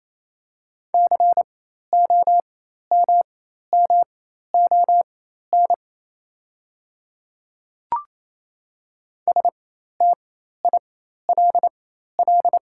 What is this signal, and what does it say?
Morse code, heard as a keyed tone, about 28 wpm, spaced as for 13 wpm, 700 Hz. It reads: COMMON STILL
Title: 4x Character Spacing - Top 500 Words - 28wpm